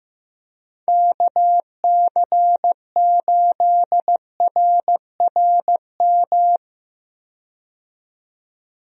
Morse code, heard as a keyed tone, about 15 wpm, 700 Hz. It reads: KC8RRM
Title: USA Call Signs - 6 Characters - 15wpm